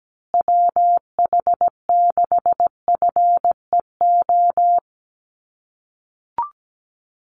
Morse code, heard as a keyed tone, about 17 wpm, 700 Hz. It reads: WH6FEO
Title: USA Call Signs Repeated 3x 17wpm